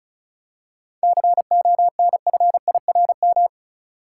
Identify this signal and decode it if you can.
Morse code, heard as a keyed tone, about 36 wpm, 700 Hz. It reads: CONFIRM
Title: Seven Letter Words - 35wpm